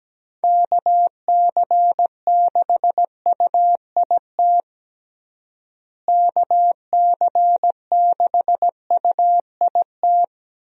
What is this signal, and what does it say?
Morse code, heard as a keyed tone, about 17 wpm, 700 Hz. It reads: KC6UIT KC6UIT
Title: USA Call Signs - 6 Characters - Repeated 2x - 17wpm